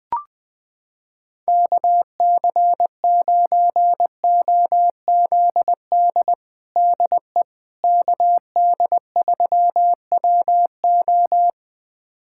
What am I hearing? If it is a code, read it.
Morse code, heard as a keyed tone, about 20 wpm, 700 Hz. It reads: KC9OZD DE KD3WO
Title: USA Callsign DE Callsign 20wpm